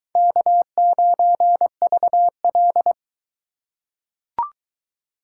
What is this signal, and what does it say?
Morse code, heard as a keyed tone, about 20 wpm, 700 Hz. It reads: K9VL